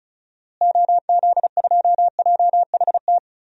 Morse code, heard as a keyed tone, about 34 wpm, 700 Hz. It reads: OZ2JHT
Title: DXCC-100 Call Signs - 6 Characters - Repeated 2x - 35wpm